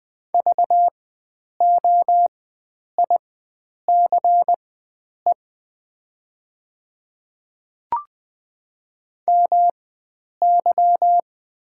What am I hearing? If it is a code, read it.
Morse code, heard as a keyed tone, about 20 wpm, spaced as for 9 wpm, 700 Hz. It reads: VOICE MY
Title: 4x Character Spacing - Top 500 Words - 20wpm